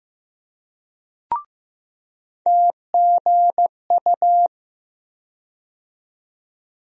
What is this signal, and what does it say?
Morse code, heard as a keyed tone, about 15 wpm, 700 Hz. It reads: TGU
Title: USA Call Sign Postfixes - 3 Letters 15wpm